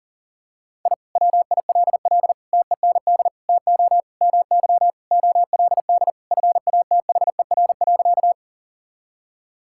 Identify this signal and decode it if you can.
Morse code, heard as a keyed tone, about 37 wpm, 700 Hz. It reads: I WILL TEND TO MY OLD FATHER.
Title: Sentences from Top 1000 Words - Review Part 5 - 40wpm